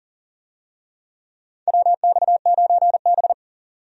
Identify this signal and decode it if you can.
Morse code, heard as a keyed tone, about 40 wpm, 700 Hz. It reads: WX9B
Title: USA Call Signs - 4 Characters - 40wpm